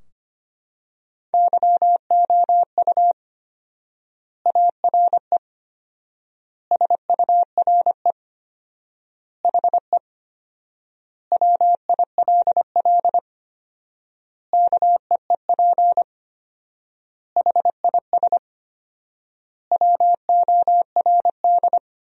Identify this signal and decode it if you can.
Morse code, heard as a keyed tone, about 25 wpm, 700 Hz. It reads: YOU ARE SURE HE WILL KEEP HIS WORD
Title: 4x Word-Spacing - Sentences from Top 300 Words - 25wpm